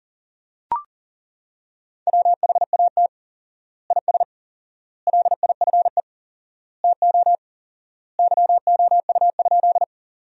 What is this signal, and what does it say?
Morse code, heard as a keyed tone, about 37 wpm, 700 Hz. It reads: WHAT IS LIFE TO YOU?
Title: 4x Word-Spacing - Sentences from Top 300 Words - 40wpm